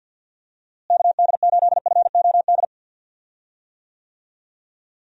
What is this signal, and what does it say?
Morse code, heard as a keyed tone, about 50 wpm, 700 Hz. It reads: KD8FOD
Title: USA Call Signs - 6 Characters - Repeated 2x - 50wpm